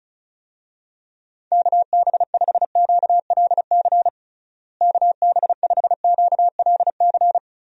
Morse code, heard as a keyed tone, about 34 wpm, 700 Hz. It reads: KB5QLC KB5QLC
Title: USA Call Signs - 6 Characters - Repeated 2x - 35wpm